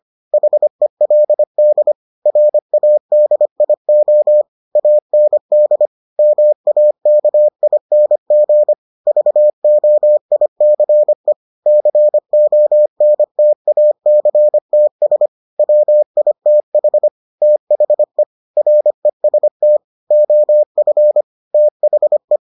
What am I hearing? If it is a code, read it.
Morse code, heard as a keyed tone, about 25 wpm, 600 Hz. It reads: HELD RADIO AND MAKING VOICE CONTACTS WITH THE REST OF THE